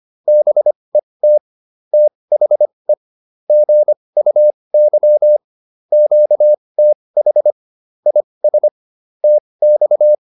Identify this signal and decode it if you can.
Morse code, heard as a keyed tone, about 25 wpm, spaced as for 20 wpm, 600 Hz. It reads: BET THE GUY QTH IS TX